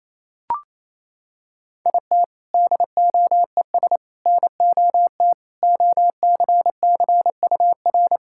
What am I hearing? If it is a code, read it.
Morse code, heard as a keyed tone, about 28 wpm, 700 Hz. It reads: IT DOES NOT OCCUR
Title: Sentences from Top 1600 Words 28wpm